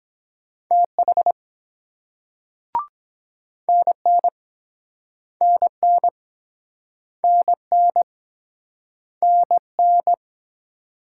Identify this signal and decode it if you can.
Morse code, heard as a keyed tone, about 24 wpm, 700 Hz. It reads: TH NN NN NN NN